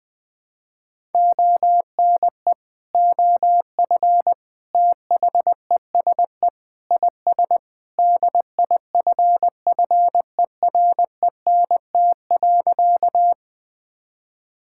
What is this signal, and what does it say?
Morse code, heard as a keyed tone, about 20 wpm, 700 Hz. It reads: ONE OF THESE IS DIFFERENT.
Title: Sentences with 2-Word Phrases - 20wpm